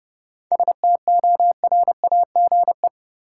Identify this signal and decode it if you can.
Morse code, heard as a keyed tone, about 30 wpm, 700 Hz. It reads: STORAGE
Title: Seven Letter Words - 30wpm